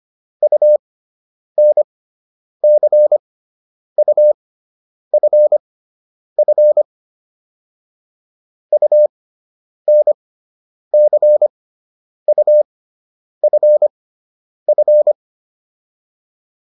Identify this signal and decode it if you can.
Morse code, heard as a keyed tone, about 25 wpm, spaced as for 9 wpm, 600 Hz. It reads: UNCUFF UNCUFF